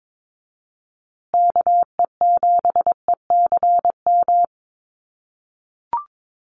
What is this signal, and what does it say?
Morse code, heard as a keyed tone, about 22 wpm, 700 Hz. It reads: KE7ECM
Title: USA Call Signs Repeated 2x 22wpm